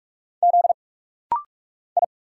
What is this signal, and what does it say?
Morse code, heard as a keyed tone, about 44 wpm, 700 Hz. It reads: Z I
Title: Single Letter-Number Alt - Rapid-Fire 45wpm